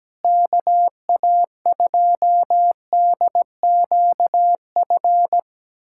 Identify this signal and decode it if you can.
Morse code, heard as a keyed tone, about 17 wpm, 700 Hz. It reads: KA2DQF
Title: USA Callsign DE Callsign - 6 Characters - 17wpm